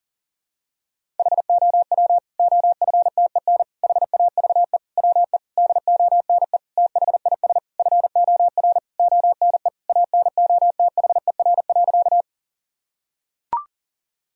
Morse code, heard as a keyed tone, about 40 wpm, 700 Hz. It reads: HOW OFTEN HAVE WE DONE THIS FOR ONE ANOTHER.
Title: Sentences from Top 1000 Words - Review Part 1 - 40wpm